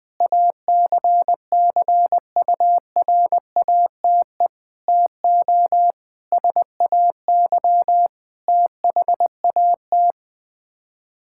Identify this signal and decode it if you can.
Morse code, heard as a keyed tone, about 20 wpm, 700 Hz. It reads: ACCURATE TO SAY THAT
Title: Sets of 8 Words - 20wpm